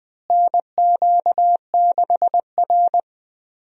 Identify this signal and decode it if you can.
Morse code, heard as a keyed tone, about 20 wpm, 700 Hz. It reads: NQ6R
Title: USA Call Signs - 4 Characters - Repeated 2x - 20wpm